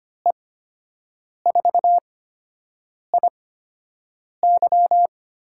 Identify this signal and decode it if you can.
Morse code, heard as a keyed tone, about 25 wpm, spaced as for 7 wpm, 700 Hz. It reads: E4IY
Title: USA Call Signs - 6 Characters - 8x Character Spacing - 25wpm